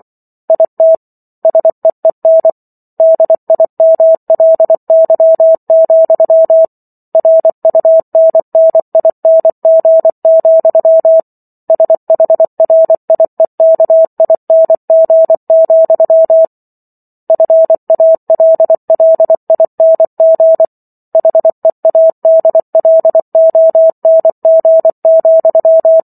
Morse code, heard as a keyed tone, about 24 wpm, 650 Hz. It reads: IT SEEN DIMLY, RUNNING, SHRIEKING, FALLING HEADLONG,